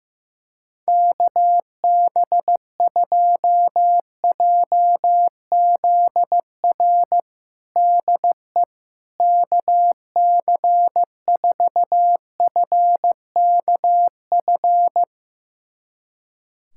Morse code, heard as a keyed tone, about 15 wpm, 700 Hz. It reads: KB2JZR DE KC4FKF
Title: USA Callsign DE Callsign - 6 Characters - 15wpm